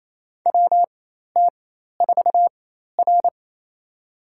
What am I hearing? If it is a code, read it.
Morse code, heard as a keyed tone, about 27 wpm, spaced as for 13 wpm, 700 Hz. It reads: WT4R
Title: USA Call Signs - 4 Characters - 4x Character Spacing - 28wpm